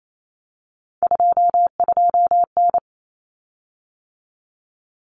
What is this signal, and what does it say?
Morse code, heard as a keyed tone, about 28 wpm, 700 Hz. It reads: 22N
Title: Common Cut Numbers - 3-Digit 28wpm